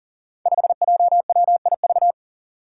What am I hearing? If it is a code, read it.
Morse code, heard as a keyed tone, about 41 wpm, 700 Hz. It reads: 5JWIV
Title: Five Challenging Letter-Number Combinations 40wpm